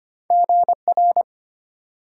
Morse code, heard as a keyed tone, about 25 wpm, 700 Hz. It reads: GR
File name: Two Letter-Number Combinations - 25wpm